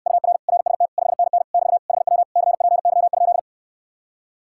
Morse code, heard as a keyed tone, about 69 wpm, 700 Hz. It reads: UR RST 5NN <BT> HW CPY?